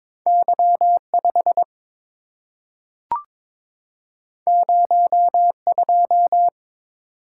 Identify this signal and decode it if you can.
Morse code, heard as a keyed tone, about 22 wpm, 700 Hz. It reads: Y5 02
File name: practice / 22wpm / Two Letter-Number Combinations - 22wpm